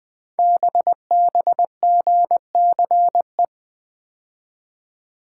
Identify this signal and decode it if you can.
Morse code, heard as a keyed tone, about 21 wpm, 700 Hz. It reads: BBGCE
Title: Code Groups - 5 Letter:Numbers - 20wpm